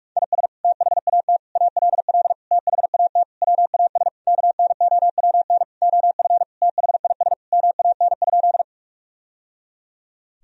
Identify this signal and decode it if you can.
Morse code, heard as a keyed tone, about 46 wpm, 700 Hz. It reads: IS THAT ALL THAT WAS KNOWN OF THIS MAN?
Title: Sentences from Top 1000 Words - Review Part 3 - 45wpm